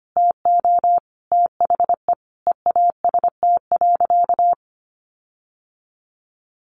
Morse code, heard as a keyed tone, about 25 wpm, 700 Hz. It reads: TO THE EAST.
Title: Sentences from Top 1300 Words 25wpm